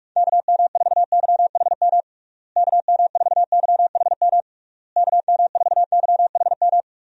Morse code, heard as a keyed tone, about 41 wpm, 700 Hz. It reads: KM4YHM KM4YHM KM4YHM
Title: USA Call Signs - 6 Characters - Repeated 3x - 45wpm